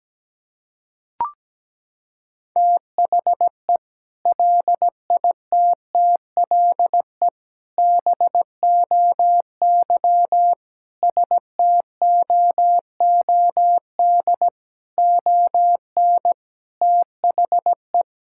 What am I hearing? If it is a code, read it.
Morse code, heard as a keyed tone, about 17 wpm, 700 Hz. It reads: THE LITTLE BOY STOOD ON THE
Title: Sentences from Top 1600 Words 17wpm